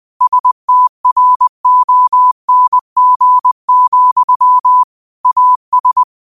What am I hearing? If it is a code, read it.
Morse code, heard as a keyed tone, about 20 wpm, 1000 Hz. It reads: STRONG, AS
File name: qotdfiles / 20wpm